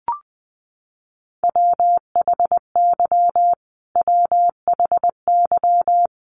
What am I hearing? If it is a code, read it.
Morse code, heard as a keyed tone, about 20 wpm, 700 Hz. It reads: WHY WHY
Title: Top 200 Words Repeated 20wpm